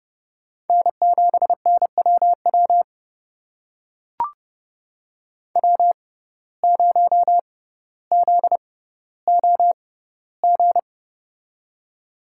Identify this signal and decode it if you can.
Morse code, heard as a keyed tone, about 32 wpm, spaced as for 16 wpm, 700 Hz. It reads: N7NWW W0ZOG
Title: USA Call Signs - 5 Characters - 6x Character Spacing - 30wpm